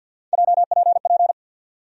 Morse code, heard as a keyed tone, about 47 wpm, 700 Hz. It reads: JPP
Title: USA Call Sign Postfixes - 3 Letters 50wpm